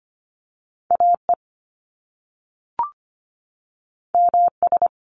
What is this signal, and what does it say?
Morse code, heard as a keyed tone, about 25 wpm, 700 Hz. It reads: AE MS
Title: Top Two Letter Combinations in Words 25wpm